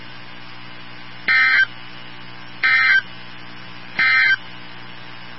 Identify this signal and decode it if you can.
Emergency Alert System (SAME) tones, heard as harsh buzzing bursts. It is heard from 1.2 to 4.4 s.
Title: NOAA Weather Radio 162.550